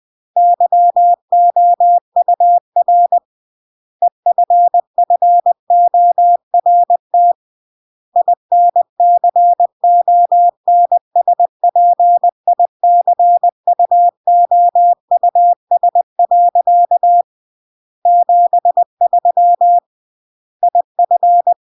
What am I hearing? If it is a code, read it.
Morse code, heard as a keyed tone, about 20 wpm, 700 Hz. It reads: YOUR EFFORT INCONSPICUOUS. 73 IF
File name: CW / jodorowsky / 82 commandments 20-20